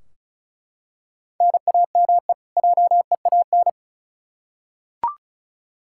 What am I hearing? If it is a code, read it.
Morse code, heard as a keyed tone, about 34 wpm, 700 Hz. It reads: NAME JEAN
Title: Practice QSO - Exchange 1 - Beginning - 35wpm